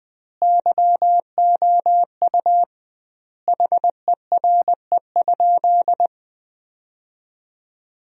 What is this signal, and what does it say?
Morse code, heard as a keyed tone, about 20 wpm, 700 Hz. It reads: YOU HERE?
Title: 2x Word-Spacing - Sentences from Top 500 Words - 20wpm